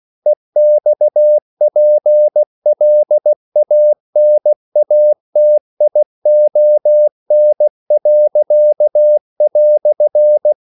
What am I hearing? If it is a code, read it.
Morse code, heard as a keyed tone, about 16 wpm, 600 Hz. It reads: EXPLANATION."
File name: CW / Practice / EnglishQuotations0216